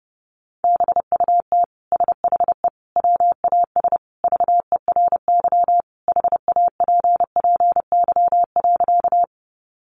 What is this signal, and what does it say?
Morse code, heard as a keyed tone, about 30 wpm, 700 Hz. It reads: BUT SHE WAS VERY HAPPY.
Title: Sentences from Top 1000 Words 30wpm